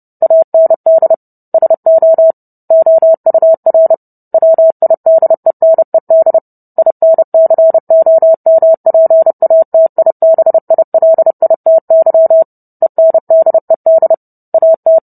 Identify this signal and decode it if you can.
Morse code, heard as a keyed tone, about 30 wpm, 650 Hz. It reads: AND SO OUR WIDENED INCOMPATIBILITY ENDED AT